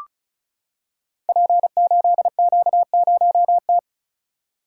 Morse code, heard as a keyed tone, about 36 wpm, 700 Hz. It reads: P8Q0T
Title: Code Groups - 5 Letter:Numbers - 35wpm